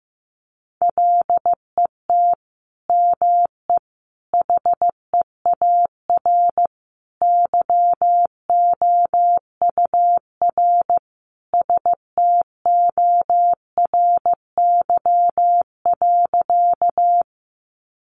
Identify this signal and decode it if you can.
Morse code, heard as a keyed tone, about 15 wpm, 700 Hz. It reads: LET ME HEAR YOUR STORY.
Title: Sentences from Top 300 Words 15wpm